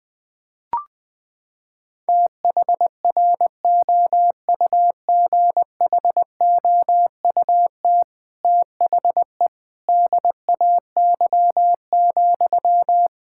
Morse code, heard as a keyed tone, about 21 wpm, 700 Hz. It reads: THROUGHOUT THE DAY,
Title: Sentences from Top 1000 Words - Review Part 5 - 20wpm